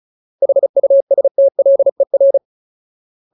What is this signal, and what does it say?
Morse code, heard as a keyed tone, about 34 wpm, 550 Hz. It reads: HUSTLER
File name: Ultimate Word List / 35WPM Ultimate